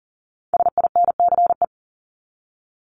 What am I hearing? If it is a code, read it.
Morse code, heard as a keyed tone, about 40 wpm, 700 Hz. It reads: SINCE
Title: Top 500 Words 40wpm